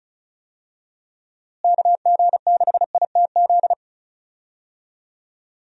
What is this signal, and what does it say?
Morse code, heard as a keyed tone, about 34 wpm, 700 Hz. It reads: KG6ITZ
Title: USA Call Signs - 6 Characters - Repeated 2x - 35wpm